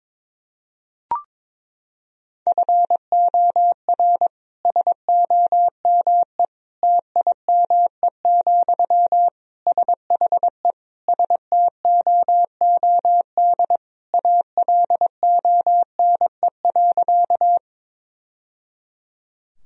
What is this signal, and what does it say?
Morse code, heard as a keyed tone, about 22 wpm, 700 Hz. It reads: FOR SOME TIME, SHE STOOD ALONE.
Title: Sentences from Top 1000 Words - Review Part 2 - 22wpm